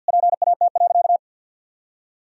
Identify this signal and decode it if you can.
Morse code, heard as a keyed tone, about 51 wpm, 700 Hz. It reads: PAT.
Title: Pangram Challenge v2 - 50wpm